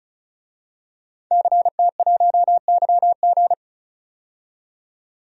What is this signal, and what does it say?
Morse code, heard as a keyed tone, about 36 wpm, 700 Hz. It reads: CT1YG
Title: DXCC-100 Call Signs - 5 Characters - 35wpm